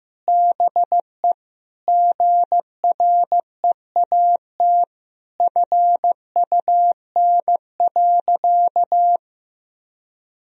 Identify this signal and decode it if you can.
Morse code, heard as a keyed tone, about 15 wpm, 700 Hz. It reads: BE GREAT FUN.
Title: Sentences from Top 1000 Words - Review Part 4 - 15wpm